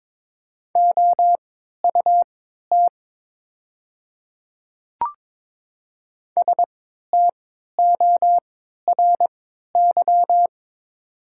Morse code, heard as a keyed tone, about 22 wpm, spaced as for 12 wpm, 700 Hz. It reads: OUT STORY
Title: 3x Character Spacing - Top 300 Words - 22wpm